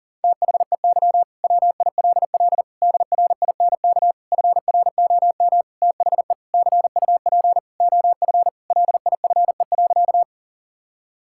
Morse code, heard as a keyed tone, about 37 wpm, 700 Hz. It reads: THEY WILL DRINK FROM THE CUP OF LIFE.
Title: Sentences from Top 1600 Words 40wpm